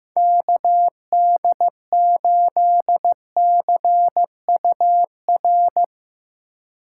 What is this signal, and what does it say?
Morse code, heard as a keyed tone, about 15 wpm, 700 Hz. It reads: KD8CUR
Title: USA Call Signs - 3x Character Spacing - 15wpm